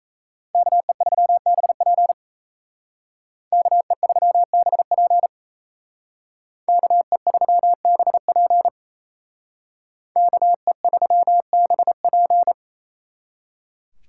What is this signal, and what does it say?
Morse code, heard as a keyed tone, about 35 wpm, 700 Hz. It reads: KE3BP KE3BP KE3BP KE3BP